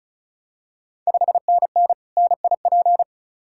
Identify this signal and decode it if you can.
Morse code, heard as a keyed tone, about 36 wpm, 690 Hz. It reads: 5NN NIP
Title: Ontario QSO Party Exchange 35wpm